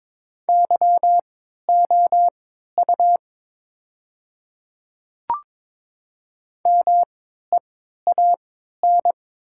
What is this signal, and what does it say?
Morse code, heard as a keyed tone, about 22 wpm, spaced as for 13 wpm, 700 Hz. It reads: YOU MEAN
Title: 3x Character Spacing - Top 200 Words - 22wpm